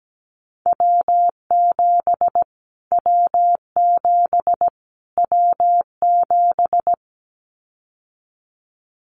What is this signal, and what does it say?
Morse code, heard as a keyed tone, about 17 wpm, 700 Hz. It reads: W7 W7 W7
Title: Callsign Prefixes for Top 25 DXCC - Repeated 3x - 17wpm